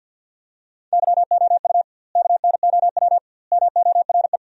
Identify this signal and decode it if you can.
Morse code, heard as a keyed tone, about 50 wpm, 700 Hz. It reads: YOU KNOW MORE
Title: 2x Word-Spacing - Sentences from Top 100 Words - 50wpm